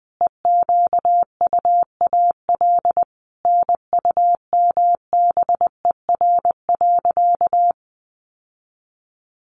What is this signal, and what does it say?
Morse code, heard as a keyed tone, about 20 wpm, 700 Hz. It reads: EQUAL NUMBER.
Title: Sentences from Top 1400 Words 20wpm